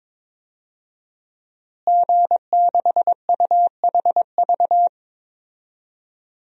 Morse code, heard as a keyed tone, about 21 wpm, 700 Hz. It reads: G6UHV
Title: DXCC-100 Call Signs - 5 Characters - Repeated 3x - 22wpm